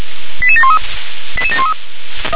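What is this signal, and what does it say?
Five-tone selective call (ZVEI), heard as quick runs of five tones, tones of about 80 ms. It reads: 88713 88713
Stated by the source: ZVEI five-tone selective call received by SDR.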